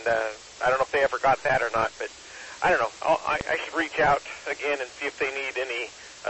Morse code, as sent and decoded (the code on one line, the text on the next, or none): none